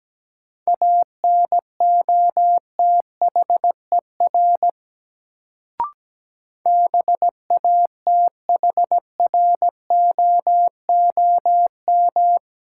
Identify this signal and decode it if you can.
Morse code, heard as a keyed tone, about 17 wpm, 700 Hz. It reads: ANOTHER BATHROOM
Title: Sets of 5 Words Encore - New Word Drill - 17wpm